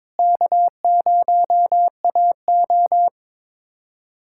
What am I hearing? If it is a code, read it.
Morse code, heard as a keyed tone, about 22 wpm, 700 Hz. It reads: K0AO
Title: USA Callsign DE Callsign 22wpm